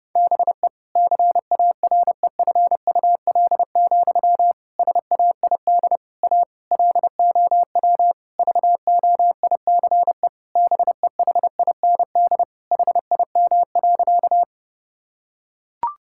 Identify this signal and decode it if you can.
Morse code, heard as a keyed tone, about 32 wpm, 700 Hz. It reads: BE CAREFUL, SAID A LOW VOICE BEHIND HIM.
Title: Sentences from Top 1600 Words 30wpm